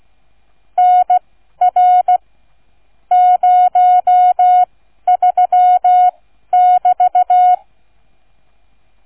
Morse code, heard as a keyed tone, about 16 wpm, spaced as for 12 wpm, 730 Hz. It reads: NR 03<BT>